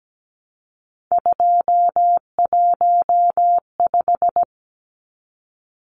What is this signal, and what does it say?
Morse code, heard as a keyed tone, about 17 wpm, 700 Hz. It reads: 215
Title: Numbers - 3-Digit 17wpm